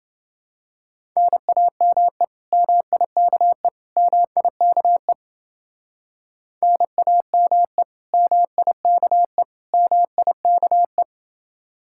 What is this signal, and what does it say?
Morse code, heard as a keyed tone, about 28 wpm, 700 Hz. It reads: NAME MIKE MIKE NAME MIKE MIKE